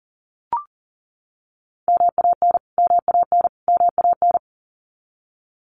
Morse code, heard as a keyed tone, about 40 wpm, 700 Hz. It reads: MAN MAN MAN
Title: Top 500 Words Repeated 40wpm